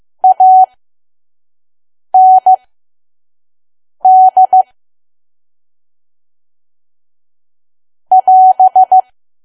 Morse code, heard as a keyed tone, about 15 wpm, spaced as for 5 wpm, 750 Hz. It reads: AND <AS>
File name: Archive / 5 WPM files / PM